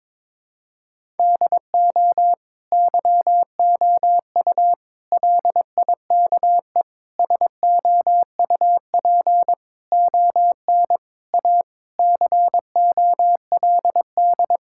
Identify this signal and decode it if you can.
Morse code, heard as a keyed tone, about 22 wpm, 700 Hz. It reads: DO YOU LIKE SOUP ON A COLD
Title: Sentences from Top 1000 Words - Review Part 6 - 22wpm